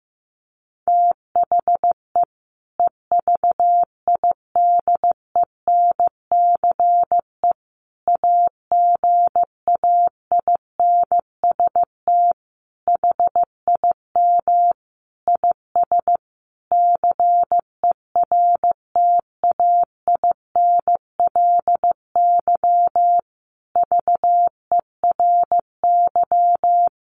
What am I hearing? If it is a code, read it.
Morse code, heard as a keyed tone, about 15 wpm, 700 Hz. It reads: THE EVIDENCE AGAINST HIM IS CERTAINLY VERY
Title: Sentences from Top 1300 Words 15wpm